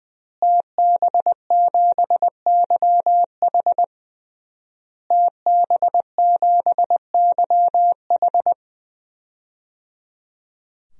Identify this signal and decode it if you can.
Morse code, heard as a keyed tone, about 21 wpm, 700 Hz. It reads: TB7YH TB7YH
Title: DXCC-100 Call Signs - 5 Characters - Repeated 3x - 20wpm